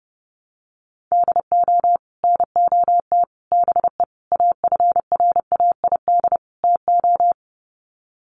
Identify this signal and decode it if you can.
Morse code, heard as a keyed tone, about 30 wpm, 700 Hz. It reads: DO NOT BE AFRAID TO
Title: Sets of 5 Words Encore2 30wpm